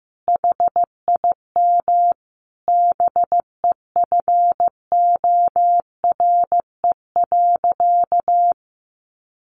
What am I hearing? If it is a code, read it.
Morse code, heard as a keyed tone, about 15 wpm, 700 Hz. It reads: HIM BEFORE.
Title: Sentences from Top 1100 Words 15wpm